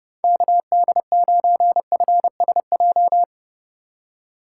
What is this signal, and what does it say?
Morse code, heard as a keyed tone, about 29 wpm, 700 Hz. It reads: KD9FSJ